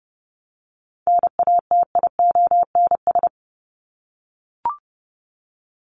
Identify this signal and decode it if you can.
Morse code, heard as a keyed tone, about 30 wpm, 700 Hz. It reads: NATIONS
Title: ES:S Suffix - Top 100 Words with Base Words 30wpm